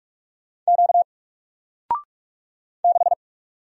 Morse code, heard as a keyed tone, about 46 wpm, 700 Hz. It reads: Q 6
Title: Single Letter-Number 45wpm